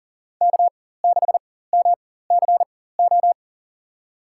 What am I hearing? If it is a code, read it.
Morse code, heard as a keyed tone, about 40 wpm, spaced as for 19 wpm, 700 Hz. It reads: K6MCO
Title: USA Call Signs - 5 Characters - 4x Character Spacing - 40wpm